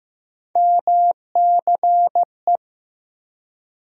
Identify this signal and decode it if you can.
Morse code, heard as a keyed tone, about 15 wpm, 700 Hz. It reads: MCE
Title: USA Call Sign Postfixes - 3 Letters 15wpm